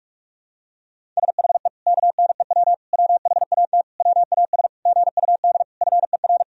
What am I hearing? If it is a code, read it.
Morse code, heard as a keyed tone, about 46 wpm, 700 Hz. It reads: SHE KNEW WHAT WAS GUD FER
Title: Sentences from Top 400 Words and Abbreviations 45wpm